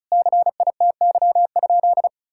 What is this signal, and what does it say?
Morse code, heard as a keyed tone, about 36 wpm, 700 Hz. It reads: CITY?
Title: Sentences from Top 1000 Words - Review Part 5 - 35wpm